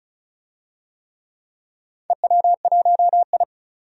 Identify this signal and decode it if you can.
Morse code, heard as a keyed tone, about 36 wpm, 700 Hz. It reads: EW1I
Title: DXCC-100 Call Signs - 4 Characters - 35wpm